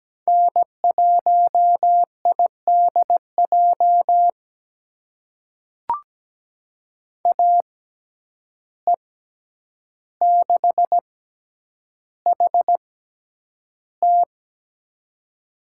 Morse code, heard as a keyed tone, about 17 wpm, spaced as for 9 wpm, 700 Hz. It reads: N1IDJ AE6HT
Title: USA Call Signs - 5 Characters - 6x Character Spacing - 17wpm